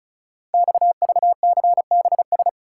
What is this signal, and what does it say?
Morse code, heard as a keyed tone, about 36 wpm, 700 Hz. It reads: XVCBS
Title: Code Groups - 5 Letters - 35wpm